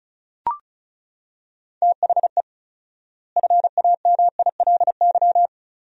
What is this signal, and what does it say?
Morse code, heard as a keyed tone, about 36 wpm, 700 Hz. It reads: THE FAMILY
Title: 4x Word-Spacing - Sentences from Top 400 Words - 35wpm